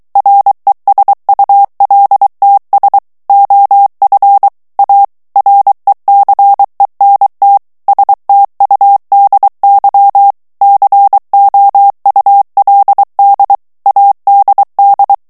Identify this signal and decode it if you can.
Morse code, heard as a keyed tone, about 23 wpm, 800 Hz. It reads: RESULTS OF A RECENT STUDY COULD ADD
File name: morse mp3 / ANGLAIS texte clair 1 vitesse 22 mots minute